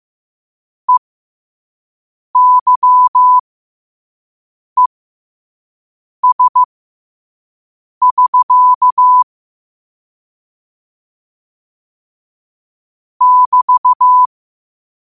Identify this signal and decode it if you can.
Morse code, heard as a keyed tone, about 15 wpm, spaced as for 5 wpm, 1000 Hz. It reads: EYES<SK> <BT>
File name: qotdfiles / 5wpm